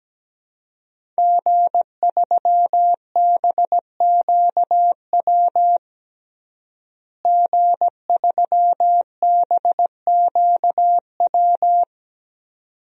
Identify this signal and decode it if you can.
Morse code, heard as a keyed tone, about 17 wpm, 700 Hz. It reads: G3BQW G3BQW
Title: DXCC-100 Call Signs - 5 Characters - Repeated 2x - 17wpm